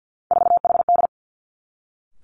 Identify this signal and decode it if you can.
Morse code, heard as a keyed tone, about 50 wpm, 700 Hz. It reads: 4HD
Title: Three Challenging Letter-Number Combinations 50wpm